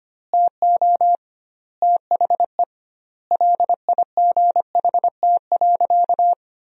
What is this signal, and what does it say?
Morse code, heard as a keyed tone, about 25 wpm, 700 Hz. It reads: TO THE LIGHT.
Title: 2x Word-Spacing - Sentences from Top 400 Words - 25wpm